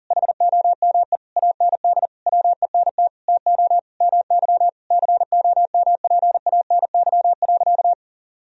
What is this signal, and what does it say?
Morse code, heard as a keyed tone, about 40 wpm, 700 Hz. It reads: HOME AND WENT TO MY COMPANY.
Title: Sentences from Top 1000 Words - Review Part 3 - 40wpm